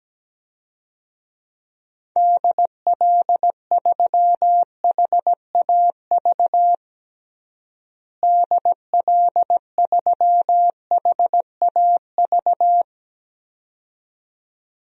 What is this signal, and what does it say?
Morse code, heard as a keyed tone, about 17 wpm, 700 Hz. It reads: DL3HAV DL3HAV
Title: DXCC-100 Call Signs - 6 Characters - Repeated 2x - 17wpm